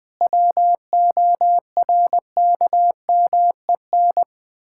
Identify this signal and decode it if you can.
Morse code, heard as a keyed tone, about 20 wpm, 700 Hz. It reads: WORKMEN
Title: Pangram Challenge v2 - 20wpm